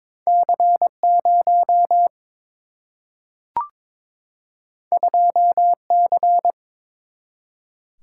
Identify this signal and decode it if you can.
Morse code, heard as a keyed tone, about 22 wpm, 700 Hz. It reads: C 0 2 C
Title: Single-Letter Single-Number 22wpm